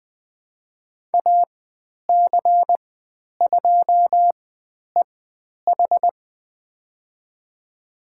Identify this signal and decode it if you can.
Morse code, heard as a keyed tone, about 20 wpm, spaced as for 10 wpm, 700 Hz. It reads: AC2EH
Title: 095 Lesson 16g - Letter c - Call Signs 20wpm